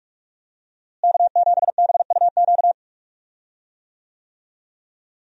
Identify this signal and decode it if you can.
Morse code, heard as a keyed tone, about 46 wpm, 700 Hz. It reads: K7BUQ
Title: USA Call Signs - 5 Characters - Repeated 2x - 45wpm